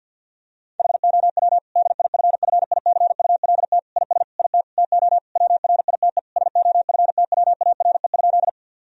Morse code, heard as a keyed tone, about 50 wpm, 700 Hz. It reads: HOW DIFFICULT IS IT TO WRITE SOFTWARE?
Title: Sentences from Top 1000 Words - Review Part 3 - 50wpm